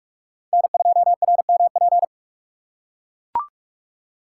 Morse code, heard as a keyed tone, about 42 wpm, 700 Hz. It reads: N2RMP
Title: USA Call Signs - 5 Characters - 6x Character Spacing - 45wpm